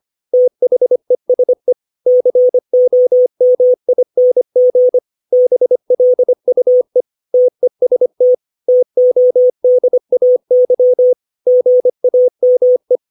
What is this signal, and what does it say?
Morse code, heard as a keyed tone, about 25 wpm, 500 Hz. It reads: THESE COMING BLUE TEST TODAY GAME
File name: 632 words / 632 Words 25WPM